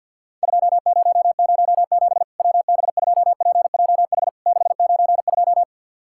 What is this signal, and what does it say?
Morse code, heard as a keyed tone, about 50 wpm, 700 Hz. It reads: 2007 WB2PJH 692
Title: Worked All Europe DX Contest - QTC Exchange with Repeats - 50wpm